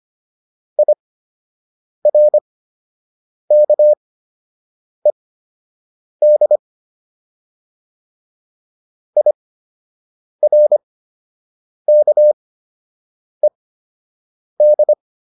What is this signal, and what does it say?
Morse code, heard as a keyed tone, about 25 wpm, spaced as for 7 wpm, 600 Hz. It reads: IRKED IRKED